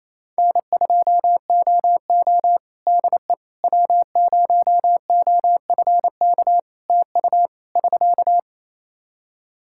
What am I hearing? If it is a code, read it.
Morse code, heard as a keyed tone, about 28 wpm, 700 Hz. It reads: N2OO DE W0OFK TU <SK>
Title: K1USN SST QSO Practice 28wpm